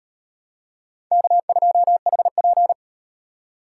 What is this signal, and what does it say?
Morse code, heard as a keyed tone, about 38 wpm, 700 Hz. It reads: K2HP